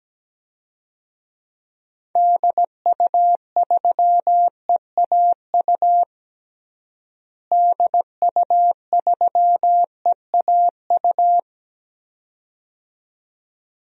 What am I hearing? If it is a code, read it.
Morse code, heard as a keyed tone, about 17 wpm, 700 Hz. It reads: DU3EAU DU3EAU
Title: DXCC-100 Call Signs - 6 Characters - Repeated 2x - 17wpm